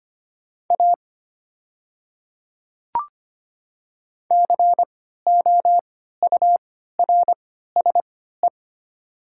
Morse code, heard as a keyed tone, about 25 wpm, spaced as for 14 wpm, 700 Hz. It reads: A COURSE
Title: 3x Character Spacing - Top 500 Words - 25wpm